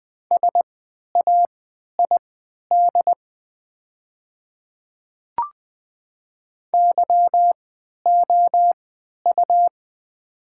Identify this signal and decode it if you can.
Morse code, heard as a keyed tone, about 20 wpm, spaced as for 11 wpm, 700 Hz. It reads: SAID YOU
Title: 3x Character Spacing - Top 300 Words - 20wpm